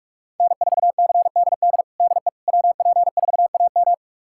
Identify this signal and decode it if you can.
Morse code, heard as a keyed tone, about 46 wpm, 700 Hz. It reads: N4CDD DE WP4AM
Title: USA Callsign DE Callsign - 5 Characters - 45wpm